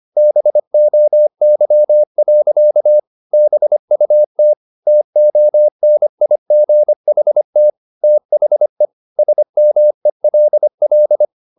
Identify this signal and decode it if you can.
Morse code, heard as a keyed tone, about 25 wpm, 600 Hz. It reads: BOY. BUT TONIGHT THE SMELL
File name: the old man and the sea part 1 25wpm